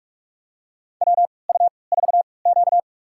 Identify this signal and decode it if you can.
Morse code, heard as a keyed tone, about 43 wpm, spaced as for 25 wpm, 700 Hz. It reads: WU4Q
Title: USA Call Signs - 4 Characters - 3x Character Spacing - 45wpm